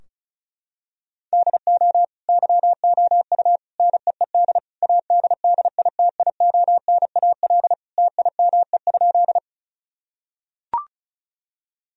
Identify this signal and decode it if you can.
Morse code, heard as a keyed tone, about 36 wpm, 700 Hz. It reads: DO YOU NEED ADDITIONAL TIME?
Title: Sentences from Top 1000 Words - Review Part 4 - 35wpm